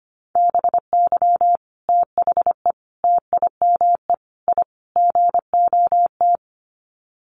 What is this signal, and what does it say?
Morse code, heard as a keyed tone, about 25 wpm, 700 Hz. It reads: BY THE TIME I GOT
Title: Sets of 5 Words Encore2 25wpm